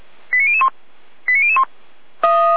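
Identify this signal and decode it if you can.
Five-tone selective call (ZVEI), heard as quick runs of five tones, tones of about 70 ms. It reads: 89001 89001
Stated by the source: ZVEI five-tone selective call received by SDR.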